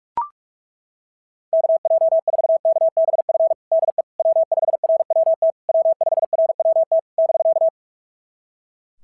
Morse code, heard as a keyed tone, about 42 wpm, 660 Hz. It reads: KJ4KBF DE W5RWT W5RWT <BK>
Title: SOTA QSO Practice 45wpm